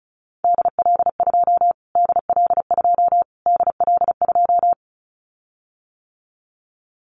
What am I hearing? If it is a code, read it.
Morse code, heard as a keyed tone, about 34 wpm, 700 Hz. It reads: DL2 DL2 DL2
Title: Callsign Prefixes for Top 25 DXCC - Repeated 3x - 35wpm